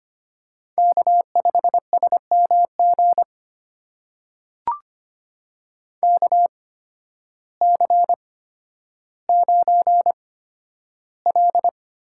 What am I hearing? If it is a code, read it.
Morse code, heard as a keyed tone, about 25 wpm, spaced as for 12 wpm, 700 Hz. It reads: K5SMG KC9L
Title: USA Call Signs - 5 Characters - 8x Character Spacing - 25wpm